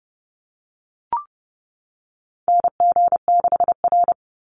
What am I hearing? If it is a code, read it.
Morse code, heard as a keyed tone, about 30 wpm, 700 Hz. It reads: NG6R
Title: NA SOTA Chaser Callsigns 30wpm